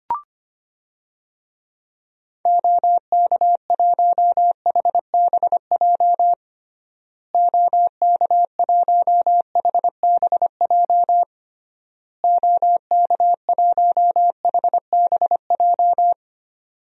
Morse code, heard as a keyed tone, about 25 wpm, 700 Hz. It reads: OK1HBJ OK1HBJ OK1HBJ
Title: DXCC-100 Call Signs - 6 Characters - Repeated 3x - 25wpm